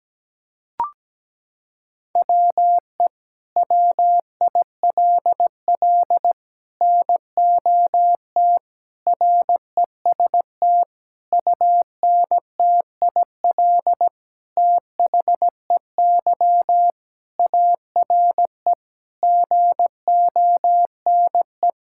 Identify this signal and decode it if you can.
Morse code, heard as a keyed tone, about 17 wpm, 700 Hz. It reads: WE WILL NOT REST UNTIL THEY ARE GONE
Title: Sentences from Top 1600 Words 17wpm